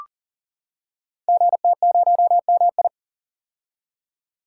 Morse code, heard as a keyed tone, about 40 wpm, 690 Hz. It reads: GT0MI
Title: Code Groups - 5 Letter:Numbers - 40wpm